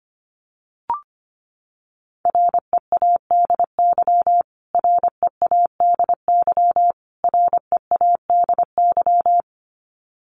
Morse code, heard as a keyed tone, about 25 wpm, 700 Hz. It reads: READY READY READY
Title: Top 500 Words Repeated 25wpm